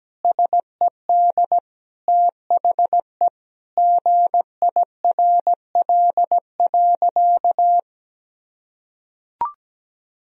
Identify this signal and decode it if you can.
Morse code, heard as a keyed tone, about 17 wpm, 700 Hz. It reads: SED THE GIRL.
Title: Sentences from Top 400 Words and Abbreviations 17wpm